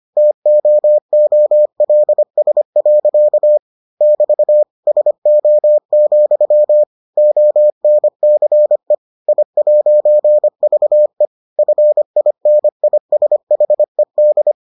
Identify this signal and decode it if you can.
Morse code, heard as a keyed tone, about 25 wpm, 600 Hz. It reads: TOOLS. <BT> SO, ONCE I'VE FINISHED